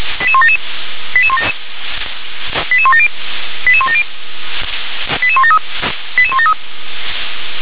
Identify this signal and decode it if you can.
Five-tone selective call (ZVEI), heard as quick runs of five tones, tones of about 70 ms. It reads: 88177 88170 88170 88173 88173